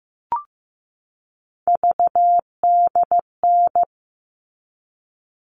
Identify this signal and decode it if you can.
Morse code, heard as a keyed tone, about 15 wpm, 700 Hz. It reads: VDN
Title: All Cut Numbers - 3-Digit 15wpm